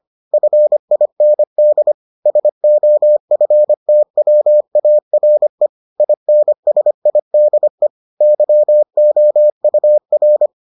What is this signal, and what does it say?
Morse code, heard as a keyed tone, about 25 wpm, 600 Hz. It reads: FIND SOFTWARE INSIDE YOUR